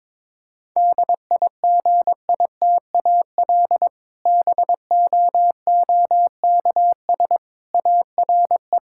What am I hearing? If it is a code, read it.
Morse code, heard as a keyed tone, about 22 wpm, 700 Hz. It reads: DIGITAL BOOKS ARE
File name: Difficult Sentences with Hints - 22wpm